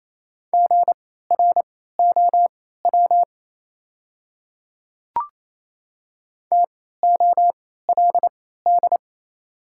Morse code, heard as a keyed tone, about 28 wpm, spaced as for 16 wpm, 690 Hz. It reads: GROW TOLD
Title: 3x Character Spacing - Top 500 Words - 28wpm